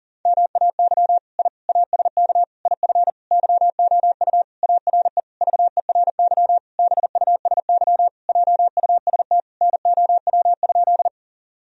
Morse code, heard as a keyed tone, about 40 wpm, 700 Hz. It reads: MAY I ASK IF YOU ARE VERY BUSY JUST NOW?
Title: Sentences from Top 1000 Words - Review Part 3 - 40wpm